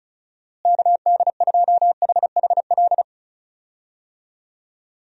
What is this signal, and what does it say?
Morse code, heard as a keyed tone, about 36 wpm, 700 Hz. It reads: KD2HHL
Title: USA Call Signs - 6 Characters - Repeated 3x - 35wpm